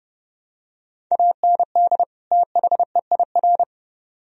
Morse code, heard as a keyed tone, about 29 wpm, 700 Hz. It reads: AND THEIR